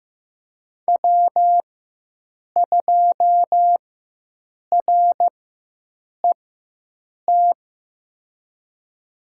Morse code, heard as a keyed tone, about 15 wpm, spaced as for 7 wpm, 700 Hz. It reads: W2RET
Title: USA Call Signs - 5 Characters - 4x Character Spacing - 15wpm